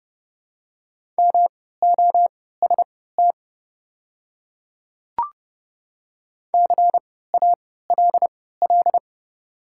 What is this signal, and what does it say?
Morse code, heard as a keyed tone, about 30 wpm, spaced as for 17 wpm, 700 Hz. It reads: MOST CALL
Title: 3x Character Spacing - Top 100 Words - 30wpm